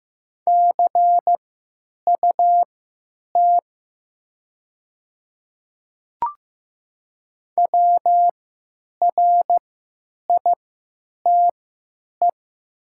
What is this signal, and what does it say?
Morse code, heard as a keyed tone, about 15 wpm, spaced as for 9 wpm, 700 Hz. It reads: CUT WRITE
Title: 3x Character Spacing - Top 300 Words - 15wpm